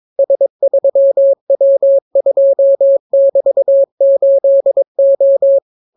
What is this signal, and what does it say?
Morse code, heard as a keyed tone, about 22 wpm, 550 Hz. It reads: S3W2<BT>8O